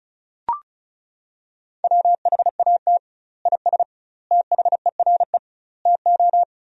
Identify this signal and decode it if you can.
Morse code, heard as a keyed tone, about 34 wpm, 700 Hz. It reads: WHAT IS THERE TO
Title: 2x Word-Spacing - Sentences from Top 100 Words - 35wpm